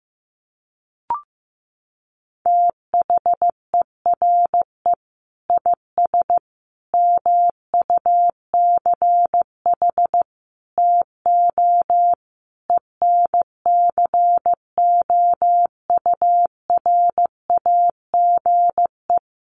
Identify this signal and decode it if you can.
Morse code, heard as a keyed tone, about 15 wpm, 700 Hz. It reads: THERE IS MUCH TO ENCOURAGE
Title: Sentences from Top 1400 Words 15wpm